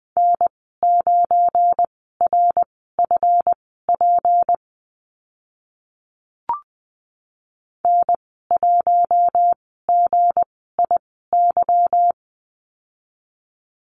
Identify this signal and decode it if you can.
Morse code, heard as a keyed tone, about 20 wpm, spaced as for 14 wpm, 700 Hz. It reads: N9RFP N1GIY
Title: 179 Lesson 29g - Slash - Call Signs 20wpm